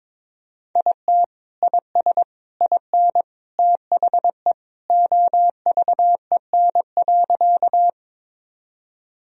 Morse code, heard as a keyed tone, about 22 wpm, 700 Hz. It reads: IT IS IN THE OVEN.
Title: Sentences from Top 1000 Words - Review Part 4 - 22wpm